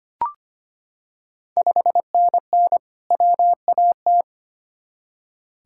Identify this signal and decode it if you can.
Morse code, heard as a keyed tone, about 25 wpm, 700 Hz. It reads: 5NN WAT
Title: Ontario QSO Party Exchange 25wpm